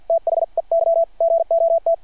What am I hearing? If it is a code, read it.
Morse code, heard as a keyed tone, about 49 wpm, spaced as for 43 wpm, 660 Hz. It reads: THEY GOT
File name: archived / CRSnet